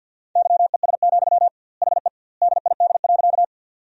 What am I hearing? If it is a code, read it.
Morse code, heard as a keyed tone, about 50 wpm, 700 Hz. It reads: YES, HE DID.
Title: 2x Word-Spacing - Sentences from Top 500 Words - 50wpm